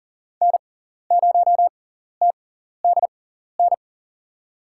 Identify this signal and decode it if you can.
Morse code, heard as a keyed tone, about 40 wpm, spaced as for 14 wpm, 700 Hz. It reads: N0TDN
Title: USA Call Signs - 6x Character Spacing - 40wpm